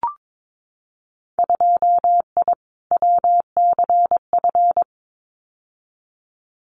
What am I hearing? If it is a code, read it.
Morse code, heard as a keyed tone, about 22 wpm, 700 Hz. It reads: 2I WCF
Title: Winter Field Day Exchanges 22wpm